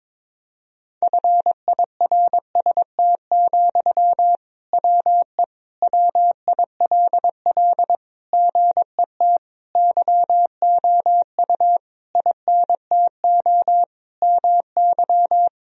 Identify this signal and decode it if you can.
Morse code, heard as a keyed tone, about 22 wpm, 700 Hz. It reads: FIRST, WE WILL GET YOU INTO MY
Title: Sentences from Top 1000 Words - Review Part 1 - 22wpm